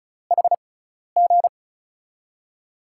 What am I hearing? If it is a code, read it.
Morse code, heard as a keyed tone, about 36 wpm, spaced as for 12 wpm, 700 Hz. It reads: HG
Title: USA Call Signs - 4 Characters - 6x Character Spacing - 35wpm